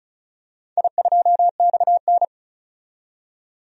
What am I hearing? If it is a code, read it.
Morse code, heard as a keyed tone, about 36 wpm, 700 Hz. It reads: I2XN
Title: DXCC-100 Call Signs - 4 Characters - 35wpm